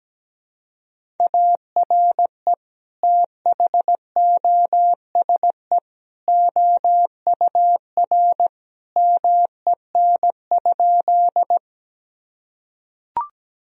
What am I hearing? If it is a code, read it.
Morse code, heard as a keyed tone, about 17 wpm, 700 Hz. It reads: ARE THOSE OUR MEN?
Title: Sentences from Top 1000 Words - Review Part 1 - 17wpm